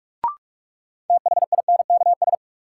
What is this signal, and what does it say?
Morse code, heard as a keyed tone, about 44 wpm, 700 Hz. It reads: THINKS
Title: Sentences from Top 1000 Words - Review Part 6 - New Word Drill - 45wpm